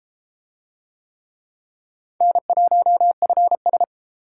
Morse code, heard as a keyed tone, about 30 wpm, 700 Hz. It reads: N1FS